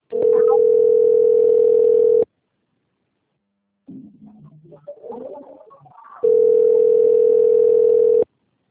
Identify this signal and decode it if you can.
Ringback tone (North America), 440 + 480 Hz.
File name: rg-1600-unknown- - - .149